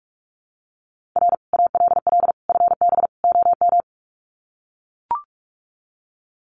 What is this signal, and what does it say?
Morse code, heard as a keyed tone, about 44 wpm, 700 Hz. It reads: R ALL FB OM
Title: Common QSO Phrases 45wpm